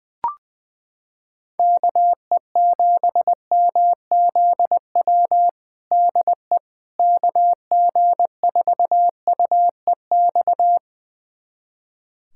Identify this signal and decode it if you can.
Morse code, heard as a keyed tone, about 20 wpm, 700 Hz. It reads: KE7MZW DE KG4UEX
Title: USA Callsign DE Callsign - 6 Characters - 20wpm